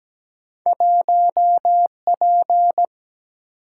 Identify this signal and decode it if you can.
Morse code, heard as a keyed tone, about 17 wpm, 700 Hz. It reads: 1P
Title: Single-Letter Single-Number 17wpm